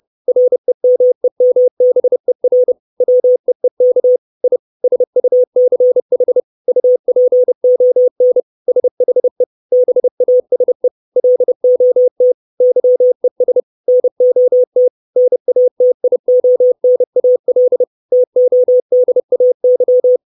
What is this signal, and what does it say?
Morse code, heard as a keyed tone, about 30 wpm, 500 Hz. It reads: REMEMBER WEEK I SUCH UPON SHE BASE LOT YES NOT NATIONAL TODAY
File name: 632 Words 30WPM